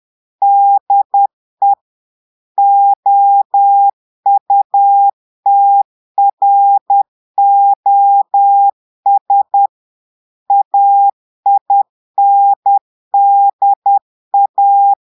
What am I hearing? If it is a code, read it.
Morse code, heard as a keyed tone, about 10 wpm, 800 Hz. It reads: DE OUTROS AINDA
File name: Portugais - texte en clair 1 - vitesse - 10 mots minute